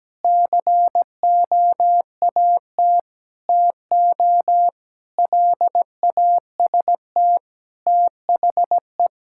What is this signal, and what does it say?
Morse code, heard as a keyed tone, about 17 wpm, 700 Hz. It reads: COAT TO LAST THE
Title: Sentences from Top 1600 Words 17wpm